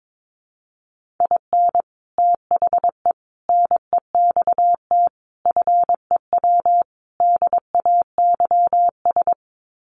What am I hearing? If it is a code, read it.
Morse code, heard as a keyed tone, about 22 wpm, 700 Hz. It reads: IN THE NEXT FEW DAYS
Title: Sets of 5 Words Encore2 22wpm